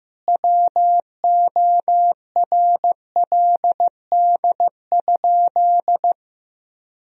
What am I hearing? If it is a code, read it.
Morse code, heard as a keyed tone, about 16 wpm, 700 Hz. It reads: WORLD?
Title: Sentences from Top 1000 Words - Review Part 1 - 15wpm